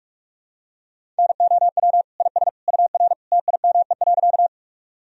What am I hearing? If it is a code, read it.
Morse code, heard as a keyed tone, about 41 wpm, 700 Hz. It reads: NOW IS UR TIME.
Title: Sentences from Top 100 Words and Abbreviations 45wpm